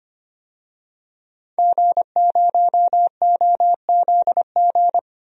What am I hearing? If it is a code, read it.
Morse code, heard as a keyed tone, about 25 wpm, 700 Hz. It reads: G0OZG
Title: DXCC-100 Call Signs - 5 Characters - 25wpm